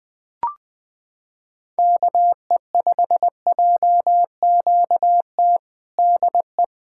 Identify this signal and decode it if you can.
Morse code, heard as a keyed tone, about 20 wpm, 700 Hz. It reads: KE5JQT DE
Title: Practice QSO - Exchange 4 - Ending - 20wpm